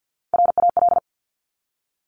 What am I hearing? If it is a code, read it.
Morse code, heard as a keyed tone, about 50 wpm, 700 Hz. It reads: RAL
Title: Top Three Letter Combinations in Words 50wpm